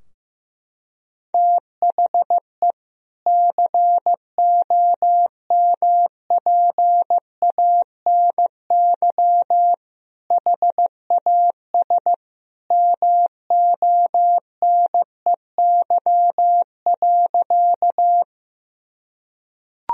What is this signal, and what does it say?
Morse code, heard as a keyed tone, about 15 wpm, 700 Hz. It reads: THE COMPANY HAS MONEY.
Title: Sentences from Top 1000 Words - Review Part 3 - 15wpm